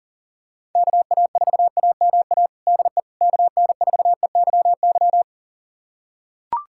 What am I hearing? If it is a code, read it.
Morse code, heard as a keyed tone, about 40 wpm, 700 Hz. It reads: KA4AMA DE KN4EYY
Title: USA Callsign DE Callsign 40wpm